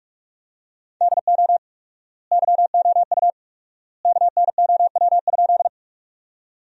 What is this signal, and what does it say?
Morse code, heard as a keyed tone, about 46 wpm, 700 Hz. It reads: DO YOU KNOW?
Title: 4x Word-Spacing - Sentences from Top 100 Words - 45wpm